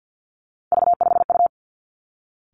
Morse code, heard as a keyed tone, about 50 wpm, 700 Hz. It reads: V5U
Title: Three Challenging Letter-Number Combinations 50wpm